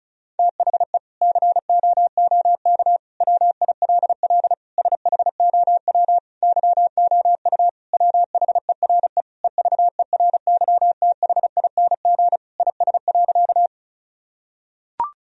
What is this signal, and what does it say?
Morse code, heard as a keyed tone, about 36 wpm, 700 Hz. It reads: THE COOK WILL SHOW YOU WHERE EVERYTHING IS.
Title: Sentences from Top 1000 Words - Review Part 4 - 35wpm